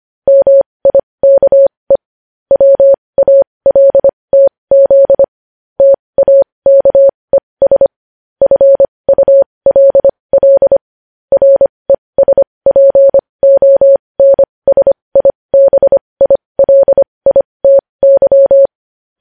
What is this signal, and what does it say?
Morse code, heard as a keyed tone, about 25 wpm, spaced as for 20 wpm, 550 Hz. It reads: MIKE WALTZ TAKES FULL RESPONSIBILITY
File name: podcast / News headlines-1x-25x20